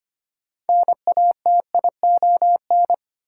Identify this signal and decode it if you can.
Morse code, heard as a keyed tone, about 25 wpm, 700 Hz. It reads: NATION
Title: Sets of 3 Words - New Word Drill - 25wpm